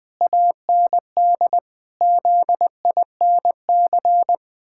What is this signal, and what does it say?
Morse code, heard as a keyed tone, about 20 wpm, 700 Hz. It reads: AND ZINC
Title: Pangram Challenge v2 - 20wpm